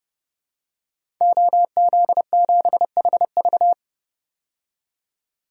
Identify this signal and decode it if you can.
Morse code, heard as a keyed tone, about 30 wpm, 700 Hz. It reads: OZ7HV
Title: DXCC-100 Call Signs - 5 Characters - 30wpm